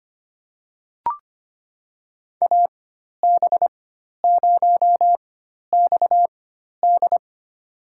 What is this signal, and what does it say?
Morse code, heard as a keyed tone, about 25 wpm, spaced as for 12 wpm, 700 Hz. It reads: AB0XD
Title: USA Call Signs - 5 Characters - 4x Character Spacing - 25wpm